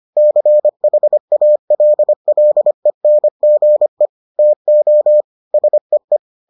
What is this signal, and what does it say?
Morse code, heard as a keyed tone, about 25 wpm, 600 Hz. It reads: CHALLENGE TO SEE